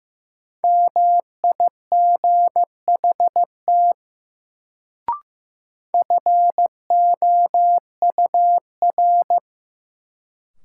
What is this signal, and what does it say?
Morse code, heard as a keyed tone, about 15 wpm, 700 Hz. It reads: MIGHT FOUR
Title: Sentences from Top 300 Words - New Word Drill - 15wpm